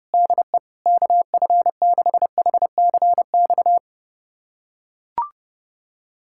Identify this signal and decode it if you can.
Morse code, heard as a keyed tone, about 30 wpm, 700 Hz. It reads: DE KF6HCX
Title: USA Callsign DE Callsign - 6 Characters - 30wpm